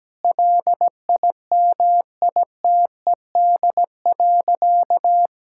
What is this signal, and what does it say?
Morse code, heard as a keyed tone, about 17 wpm, 700 Hz. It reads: LIMITED.
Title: Sentences from Top 1000 Words - Review Part 6 - 17wpm